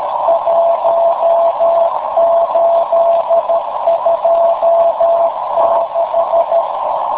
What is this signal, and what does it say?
Morse code, heard as a keyed tone, about 13 wpm, 670 Hz, with noise. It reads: 1826
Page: In the afternoon at 1130z TR0A/BCN